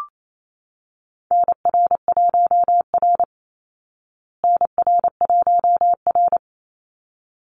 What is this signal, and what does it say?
Morse code, heard as a keyed tone, about 28 wpm, 690 Hz. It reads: NR1R NR1R
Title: Call Signs Repeated 3x - 28wpm